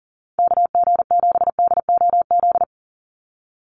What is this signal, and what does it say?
Morse code, heard as a keyed tone, about 40 wpm, 700 Hz. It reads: KG7DOZ
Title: USA Call Signs 40wpm